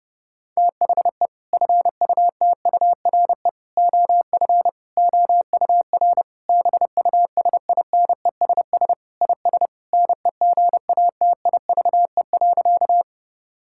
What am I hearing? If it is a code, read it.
Morse code, heard as a keyed tone, about 30 wpm, 700 Hz. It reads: THE FUTURE OF OUR BUSINESS IS NEGATIVE.
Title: Sentences from Top 1000 Words - Review Part 6 - 30wpm